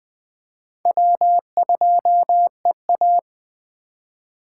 Morse code, heard as a keyed tone, about 20 wpm, 700 Hz. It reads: W2EA